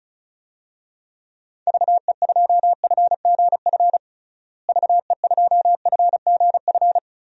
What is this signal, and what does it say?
Morse code, heard as a keyed tone, about 36 wpm, 700 Hz. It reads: VE2FGF VE2FGF
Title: DXCC-100 Call Signs - 6 Characters - Repeated 2x - 35wpm